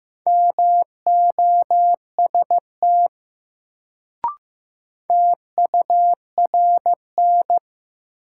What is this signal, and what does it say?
Morse code, heard as a keyed tone, about 15 wpm, 700 Hz. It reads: MOST TURN
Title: Sentences from Top 200 Words - New Word Drill - 15wpm